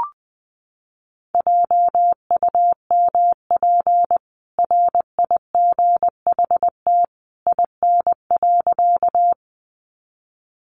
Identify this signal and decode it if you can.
Morse code, heard as a keyed tone, about 19 wpm, 700 Hz. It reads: JUMP RIGHT IN.
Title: Sentences from Top 1300 Words 20wpm